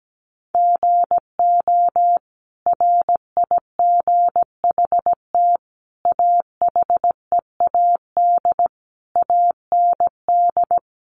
Morse code, heard as a keyed tone, about 17 wpm, 700 Hz. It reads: GO RIGHT AHEAD AND
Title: Sentences from Top 1100 Words 17wpm